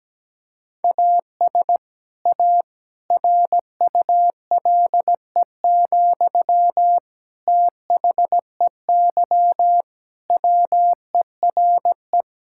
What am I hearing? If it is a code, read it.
Morse code, heard as a keyed tone, about 17 wpm, 700 Hz. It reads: AS A RULE, THEY WERE
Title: Sentences from Top 1000 Words - Review Part 6 - 17wpm